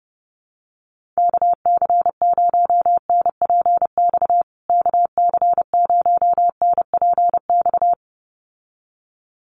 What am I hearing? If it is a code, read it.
Morse code, heard as a keyed tone, about 29 wpm, 700 Hz. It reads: KC0NPX KC0NPX
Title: USA Call Signs Repeated 2x 30wpm